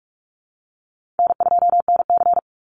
Morse code, heard as a keyed tone, about 44 wpm, 700 Hz. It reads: N2NC
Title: Call Signs Repeated 2x - 45wpm